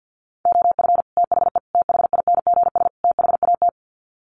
Morse code, heard as a keyed tone, about 50 wpm, 700 Hz. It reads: OF THE THINGS THAT
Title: Sets of 4 Words 50wpm